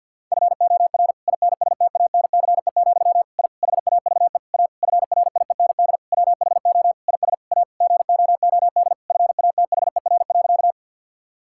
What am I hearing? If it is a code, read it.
Morse code, heard as a keyed tone, about 50 wpm, 700 Hz. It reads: FOR INSTANCE, I HAVE A FRIEND WHO IS A GOOD FATHER.
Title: Sentences from Top 1000 Words - Review Part 5 - 50wpm